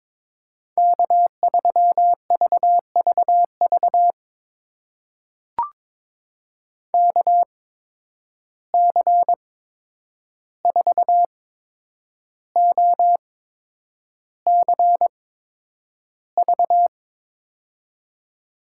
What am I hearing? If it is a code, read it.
Morse code, heard as a keyed tone, about 22 wpm, spaced as for 9 wpm, 700 Hz. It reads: K3VVV KC4OCV
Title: USA Call Signs - 8x Character Spacing - 22wpm